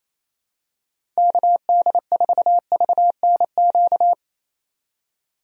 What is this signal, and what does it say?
Morse code, heard as a keyed tone, about 28 wpm, 700 Hz. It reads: KD4VNQ
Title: USA Call Signs - 6 Characters - 28wpm